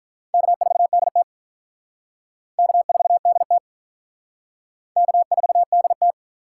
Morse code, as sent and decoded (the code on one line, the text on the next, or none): -.- ....- -.. - / -.- ....- -.. - / -.- ....- -.. -
K4DT K4DT K4DT